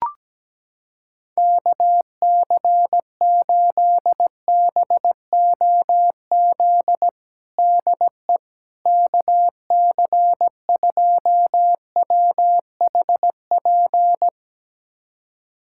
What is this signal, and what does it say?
Morse code, heard as a keyed tone, about 17 wpm, 700 Hz. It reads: KC8BOZ DE KC2WHP
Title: USA Callsign DE Callsign - 6 Characters - 17wpm